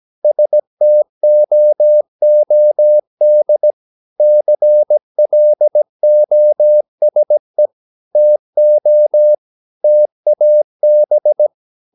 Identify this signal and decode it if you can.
Morse code, heard as a keyed tone, about 17 wpm, 600 Hz. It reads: STOOD CLOSE TO TAB
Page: Romanen Never af Ken Follet på engelsk.